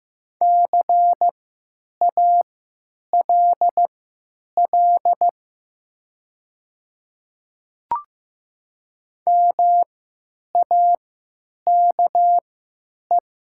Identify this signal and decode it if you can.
Morse code, heard as a keyed tone, about 15 wpm, spaced as for 9 wpm, 700 Hz. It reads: CALL MAKE
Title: 3x Character Spacing - Top 300 Words - 15wpm